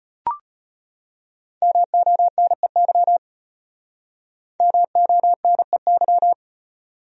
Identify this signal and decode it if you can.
Morse code, heard as a keyed tone, about 32 wpm, 700 Hz. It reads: MONEY MONEY